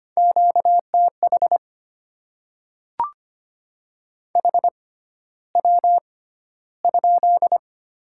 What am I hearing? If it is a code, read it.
Morse code, heard as a keyed tone, about 25 wpm, spaced as for 13 wpm, 700 Hz. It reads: QTH HW?
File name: Top 20 QSO Elements - 6x Character Spacing - 25wpm